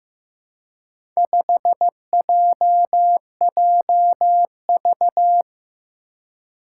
Morse code, heard as a keyed tone, about 15 wpm, 700 Hz. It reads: 5JJV
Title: Four Challenging Letter-Number Combinations 15wpm